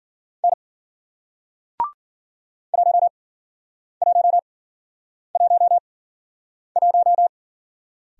Morse code, heard as a keyed tone, about 51 wpm, spaced as for 9 wpm, 700 Hz. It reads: N 1111